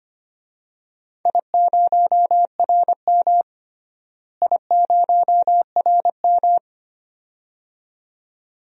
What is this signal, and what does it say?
Morse code, heard as a keyed tone, about 25 wpm, 700 Hz. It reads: I0RM I0RM
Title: DXCC-100 Call Signs - 4 Characters - Repeated 2x - 25wpm